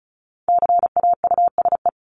Morse code, heard as a keyed tone, about 34 wpm, 700 Hz. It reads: CAUSE
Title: Top 300 Words 35wpm